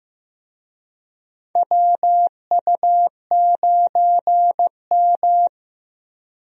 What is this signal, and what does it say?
Morse code, heard as a keyed tone, about 15 wpm, 700 Hz. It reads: WU9M
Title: USA Call Signs - 4 Characters - Repeated 2x - 15wpm